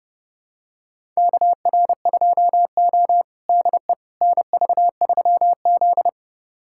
Morse code, heard as a keyed tone, about 30 wpm, 700 Hz. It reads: KR2O DE NV3Z
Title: USA Callsign DE Callsign - 4 Characters - 30wpm